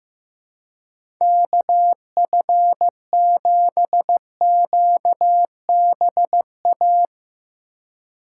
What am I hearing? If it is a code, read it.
Morse code, heard as a keyed tone, about 16 wpm, 700 Hz. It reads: KF7QBA
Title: USA Call Signs - 2x Character Spacing - 15wpm